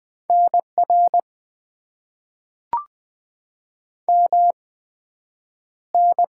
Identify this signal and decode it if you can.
Morse code, heard as a keyed tone, about 20 wpm, spaced as for 9 wpm, 700 Hz. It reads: NR MN
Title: Top 90 QSO Elements - 8x Character Spacing - 20wpm